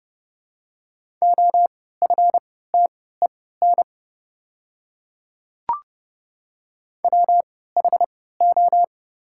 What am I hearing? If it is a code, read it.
Morse code, heard as a keyed tone, about 31 wpm, spaced as for 17 wpm, 700 Hz. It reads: OFTEN WHO
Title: 3x Character Spacing - Top 500 Words - 30wpm